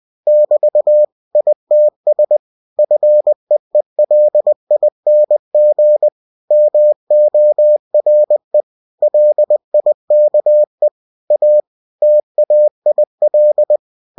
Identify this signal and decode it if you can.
Morse code, heard as a keyed tone, about 20 wpm, 600 Hz. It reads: <BT> ITS FEELING MORE LIKE A TAIL
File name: cw-books / winnie-the-pooh / winnie-20wpm-page-0088